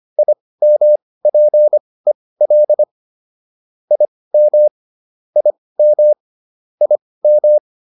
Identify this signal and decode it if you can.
Morse code, heard as a keyed tone, about 25 wpm, spaced as for 18 wpm, 600 Hz. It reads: IMPEL IM IM IM
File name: IM5 18